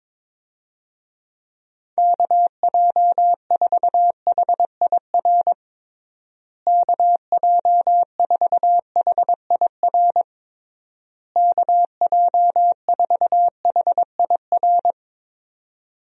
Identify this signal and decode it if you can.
Morse code, heard as a keyed tone, about 21 wpm, 700 Hz. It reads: KJ4HIR KJ4HIR KJ4HIR
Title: USA Call Signs - 6 Characters - Repeated 3x - 22wpm